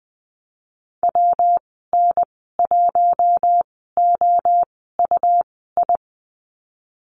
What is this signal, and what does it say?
Morse code, heard as a keyed tone, about 19 wpm, spaced as for 14 wpm, 700 Hz. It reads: WN1OUI
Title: 128 Lesson 21g - Number 6 - Call Signs 20wpm